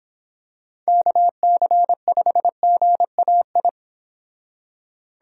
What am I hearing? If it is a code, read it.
Morse code, heard as a keyed tone, about 26 wpm, 700 Hz. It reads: KC5GAI